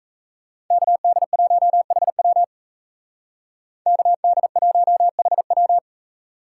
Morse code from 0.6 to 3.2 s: KD1HW

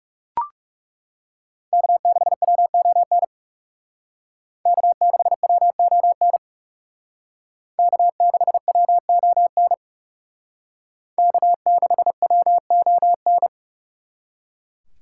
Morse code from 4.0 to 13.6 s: K6WON K6WON K6WON

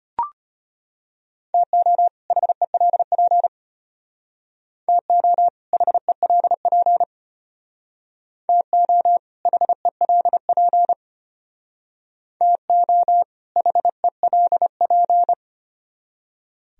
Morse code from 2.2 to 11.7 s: HELP TO HELP TO HELP

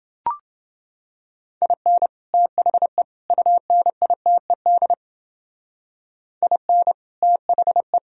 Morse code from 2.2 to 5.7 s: THE UNITED

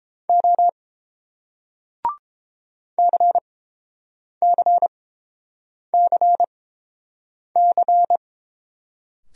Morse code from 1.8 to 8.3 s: CCCC

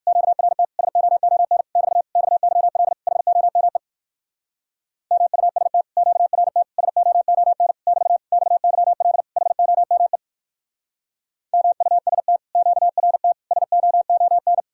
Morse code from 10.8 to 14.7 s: MUST QRT SOON

